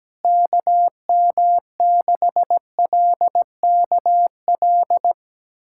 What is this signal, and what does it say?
Morse code, heard as a keyed tone, about 17 wpm, 700 Hz. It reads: KM6LKL
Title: USA Callsign DE Callsign - 6 Characters - 17wpm